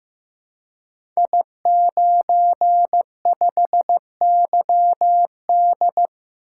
Morse code, heard as a keyed tone, about 15 wpm, 700 Hz. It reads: I95YD
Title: Code Groups - 5 Letter:Numbers - 15wpm